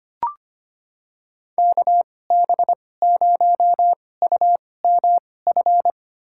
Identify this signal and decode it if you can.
Morse code, heard as a keyed tone, about 25 wpm, spaced as for 18 wpm, 700 Hz. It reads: KB0UMF
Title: USA Call Signs - 6 Characters - 2x Character Spacing - 25wpm